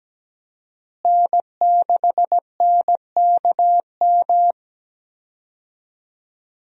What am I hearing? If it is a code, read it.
Morse code, heard as a keyed tone, about 17 wpm, 700 Hz. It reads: N6NKM
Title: USA Call Signs - 5 Characters - Repeated 2x - 17wpm